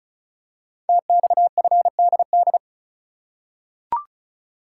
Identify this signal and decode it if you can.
Morse code, heard as a keyed tone, about 36 wpm, 700 Hz. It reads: TXFDD
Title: Code Groups - 5 Letters - 35wpm